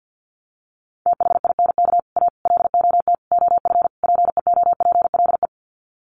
Morse code, heard as a keyed tone, about 48 wpm, 700 Hz. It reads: THINK A LOT OF PEOPLE
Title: Sets of 5 Words 50wpm